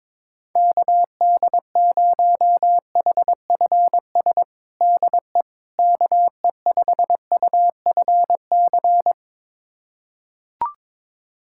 Morse code, heard as a keyed tone, about 22 wpm, 700 Hz. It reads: KD0HFS DE KE5UFC
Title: USA Callsign DE Callsign - 6 Characters - 22wpm